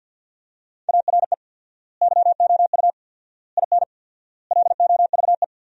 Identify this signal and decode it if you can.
Morse code, heard as a keyed tone, about 47 wpm, 700 Hz. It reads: ARE YOU IN LOVE
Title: 4x Word-Spacing - Sentences from Top 400 Words - 50wpm